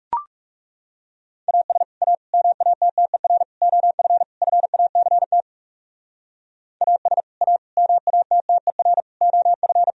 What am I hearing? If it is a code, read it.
Morse code, heard as a keyed tone, about 36 wpm, 700 Hz. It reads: AS A MATTER OF FACT AS A MATTER OF
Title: Speed-Racing - Sets of 5 Words 30wpm